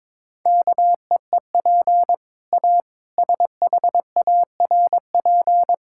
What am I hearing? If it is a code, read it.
Morse code, heard as a keyed tone, about 22 wpm, 700 Hz. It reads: KEEP A SHARP
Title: Sentences from Top 1600 Words 22wpm